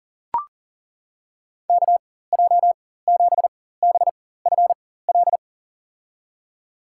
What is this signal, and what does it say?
Morse code, heard as a keyed tone, about 40 wpm, spaced as for 19 wpm, 700 Hz. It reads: KJ7BFL
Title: USA Call Signs - 4x Character Spacing - 40wpm